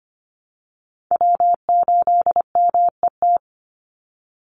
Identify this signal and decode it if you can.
Morse code, heard as a keyed tone, about 25 wpm, 700 Hz. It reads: W8MET
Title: Call Signs Repeated 2x - 25wpm